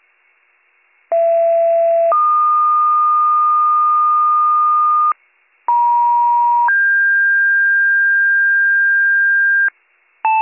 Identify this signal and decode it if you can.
Two-tone page, heard as a one-second tone, then a three-second tone, about 670 then 1220 Hz, about 950 then 1600 Hz, with noise. It is heard from 1.1 to 9.7 s.